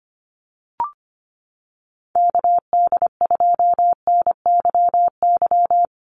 Morse code, heard as a keyed tone, about 25 wpm, 700 Hz. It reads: KD2NYY
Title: USA Call Signs 25wpm